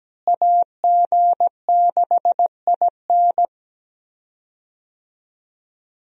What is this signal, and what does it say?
Morse code, heard as a keyed tone, about 17 wpm, 700 Hz. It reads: AG6IN
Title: USA Call Signs - 5 Characters - Repeated 3x - 17wpm